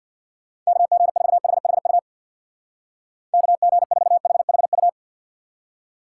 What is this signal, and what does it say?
Morse code, heard as a keyed tone, about 64 wpm, 700 Hz. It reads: KG4HHU KG4HHU